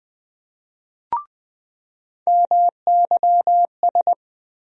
Morse code, heard as a keyed tone, about 20 wpm, 700 Hz. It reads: MYS
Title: USA Call Sign Postfixes - 3 Letters 20wpm